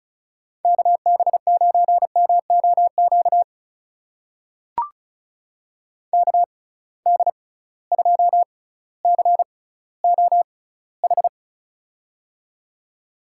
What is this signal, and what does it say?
Morse code, heard as a keyed tone, about 36 wpm, spaced as for 18 wpm, 700 Hz. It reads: KB9MOQ KD2COH
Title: USA Call Signs - 6 Characters - 6x Character Spacing - 35wpm